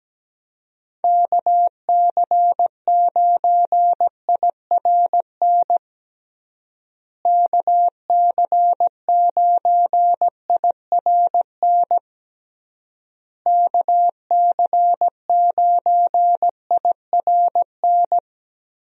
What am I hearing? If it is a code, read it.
Morse code, heard as a keyed tone, about 17 wpm, 700 Hz. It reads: KC9IRN KC9IRN KC9IRN
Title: USA Call Signs - 6 Characters - Repeated 3x - 17wpm